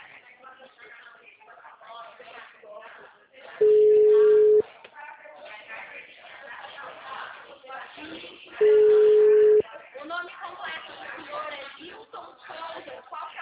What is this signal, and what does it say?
Ringback tone (Europe, 425 Hz), with speech.